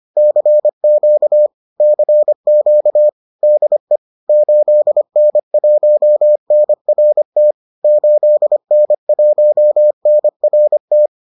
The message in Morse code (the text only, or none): CQ CQ DE 8N1NRT 8N1NRT